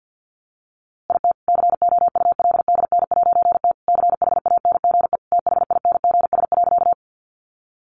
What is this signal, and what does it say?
Morse code, heard as a keyed tone, about 50 wpm, 700 Hz. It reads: IT COULDN'T CHANGE THINGS.
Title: A Sound Of Thunder 50wpm